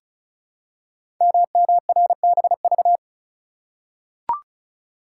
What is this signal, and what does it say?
Morse code, heard as a keyed tone, about 36 wpm, 700 Hz. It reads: MMRBV
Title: Code Groups - 5 Letters - 35wpm